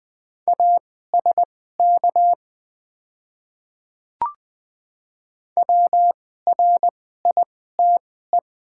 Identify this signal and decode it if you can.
Morse code, heard as a keyed tone, about 20 wpm, spaced as for 14 wpm, 700 Hz. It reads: ASK WRITE
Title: 2x Character Spacing - Top 300 Words - 20wpm